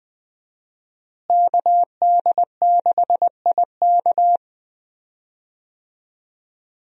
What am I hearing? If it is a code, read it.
Morse code, heard as a keyed tone, about 20 wpm, 700 Hz. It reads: KD6IK
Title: USA Call Signs - 5 Characters - Repeated 2x - 20wpm